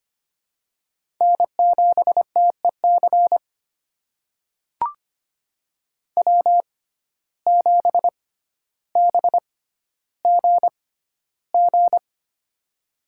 Morse code, heard as a keyed tone, about 25 wpm, spaced as for 13 wpm, 700 Hz. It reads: N7TEC W7BGG
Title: USA Call Signs - 6x Character Spacing - 25wpm